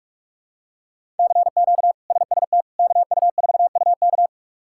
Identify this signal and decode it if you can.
Morse code, heard as a keyed tone, about 46 wpm, 700 Hz. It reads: CQ SST KU4UK
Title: K1USN SST QSO Practice - 45wpm